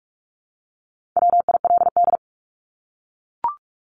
Morse code, heard as a keyed tone, about 44 wpm, 700 Hz. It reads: WILD
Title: Top 1501-1750 Words 45wpm